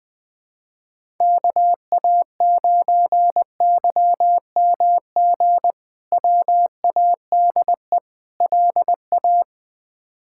Morse code, heard as a keyed tone, about 21 wpm, 700 Hz. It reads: KA9YMG WADE LA
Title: K1USN SST QSO Practice - 20wpm